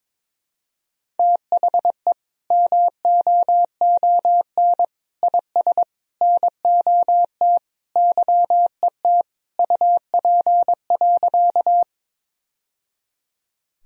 Morse code, heard as a keyed tone, about 22 wpm, 700 Hz. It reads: THE MOON IS NOT YET UP.
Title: Sentences from Top 1000 Words - Review Part 3 - 22wpm